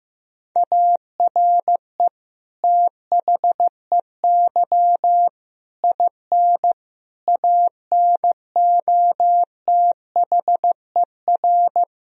Morse code, heard as a keyed tone, about 15 wpm, 700 Hz. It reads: ARE THEY IN ANOTHER
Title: Sentences from Top 1000 Words - Review Part 1 - 15wpm